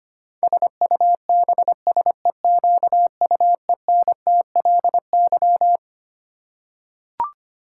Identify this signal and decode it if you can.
Morse code, heard as a keyed tone, about 25 wpm, 700 Hz. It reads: SUBSEQUENTLY
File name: Twelve Letter Words - 25wpm